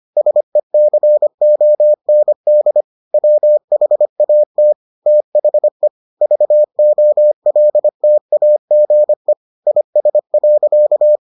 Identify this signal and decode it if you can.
Morse code, heard as a keyed tone, about 25 wpm, 600 Hz. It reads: SECOND WHAT THE VOLTAGE IS.